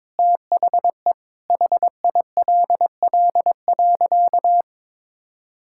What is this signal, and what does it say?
Morse code, heard as a keyed tone, about 22 wpm, 700 Hz. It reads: THE HILL.
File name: practice / 22wpm / Sentences with 4-Word Phrases - 22wpm